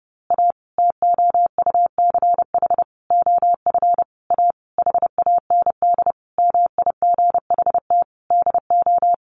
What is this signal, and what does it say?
Morse code, heard as a keyed tone, about 30 wpm, 700 Hz. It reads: A TOUCH OF A HAND MIGHT DO
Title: A Sound Of Thunder 30wpm